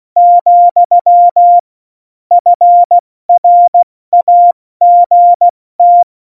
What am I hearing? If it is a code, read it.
Morse code, heard as a keyed tone, about 16 wpm, 700 Hz. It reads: , FRAGT
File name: Morsekurs / Witze / Witz 28. Januar 2020, 70 F80